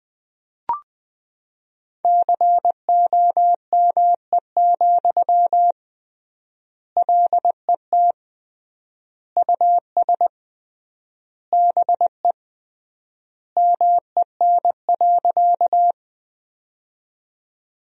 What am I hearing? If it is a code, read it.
Morse code, heard as a keyed tone, about 20 wpm, 700 Hz. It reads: COME, LET US BE MEN.
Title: 3x Word-Spacing - Sentences from Top 300 Words - 20wpm